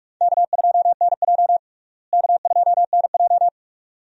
Morse code, heard as a keyed tone, about 46 wpm, 700 Hz. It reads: K2NJ K2NJ
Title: USA Call Signs - 4 Characters - Repeated 2x - 45wpm